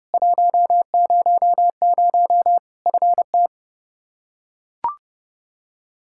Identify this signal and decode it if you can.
Morse code, heard as a keyed tone, about 30 wpm, 700 Hz. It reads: 100 FT
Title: Practice QSO - Exchange 2 - Info - 30wpm